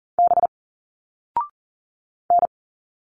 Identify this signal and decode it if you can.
Morse code, heard as a keyed tone, about 40 wpm, 700 Hz. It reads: B N
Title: Single Letters 40wpm